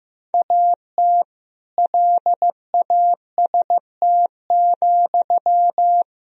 Morse code, heard as a keyed tone, about 16 wpm, 700 Hz. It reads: AT LAST,
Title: Sentences from Top 1000 Words - Review Part 1 - 15wpm